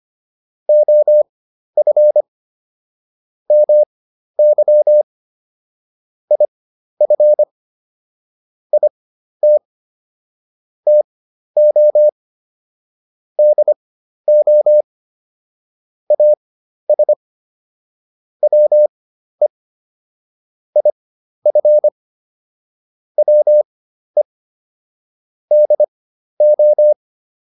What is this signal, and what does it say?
Morse code, heard as a keyed tone, about 25 wpm, spaced as for 12 wpm, 600 Hz. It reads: OF MY IF IT TO DO AS WE IF WE DO